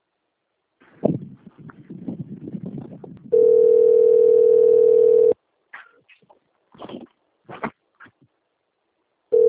Ringback tone (North America), 440 + 480 Hz.
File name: out- - - - - .869